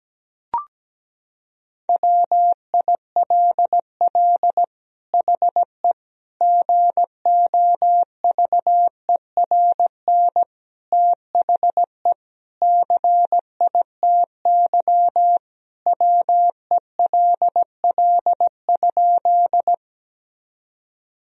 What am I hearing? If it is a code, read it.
Morse code, heard as a keyed tone, about 17 wpm, 700 Hz. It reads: WILL HE GOVERN THE CITY WELL?
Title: Sentences from Top 1000 Words - Review Part 2 - 17wpm